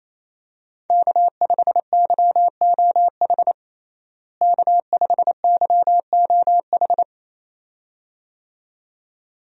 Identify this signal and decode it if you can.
Morse code, heard as a keyed tone, about 27 wpm, 700 Hz. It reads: K5YOH K5YOH
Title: USA Call Signs - 5 Characters - Repeated 2x - 28wpm